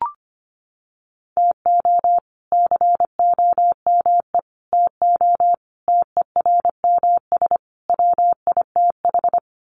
Morse code, heard as a keyed tone, about 25 wpm, 700 Hz. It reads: TO COME TO TERMS WITH
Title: Sets of 5 Words Encore 25wpm